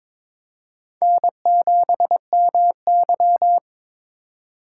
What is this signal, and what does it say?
Morse code, heard as a keyed tone, about 22 wpm, 700 Hz. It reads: N7MY
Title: USA Call Signs - 4 Characters - 4x Character Spacing - 22wpm